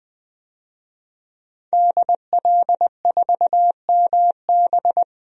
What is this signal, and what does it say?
Morse code, heard as a keyed tone, about 20 wpm, 700 Hz. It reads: DL4MB
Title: DXCC-100 Call Signs - 5 Characters - 20wpm